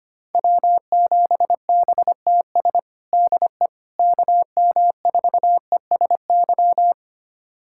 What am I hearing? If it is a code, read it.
Morse code, heard as a keyed tone, about 25 wpm, 700 Hz. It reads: W7BTS DE KM4ESY
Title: USA Callsign DE Callsign 25wpm